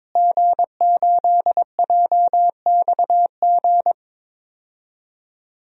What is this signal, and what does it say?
Morse code, heard as a keyed tone, about 21 wpm, 700 Hz. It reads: G8JXG
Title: DXCC-100 Call Signs - 5 Characters - Repeated 2x - 22wpm